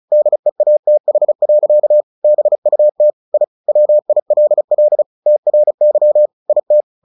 Morse code, heard as a keyed tone, about 35 wpm, 600 Hz. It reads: DEATH. BUT I WILL TRY IT
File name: the old man and the sea 2 35wpm